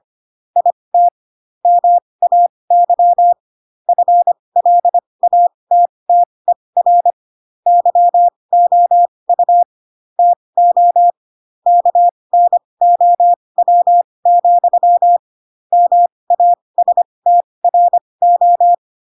Morse code, heard as a keyed tone, about 25 wpm, spaced as for 20 wpm, 700 Hz. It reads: IT MAY FLATTER YOU TO KNOW, MASTRO
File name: Pinocchio / Pinocchio ch2 15 25 ik1hge